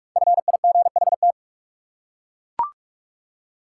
Morse code, heard as a keyed tone, about 41 wpm, 700 Hz. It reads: FIGHT
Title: Five Letter Words - 45wpm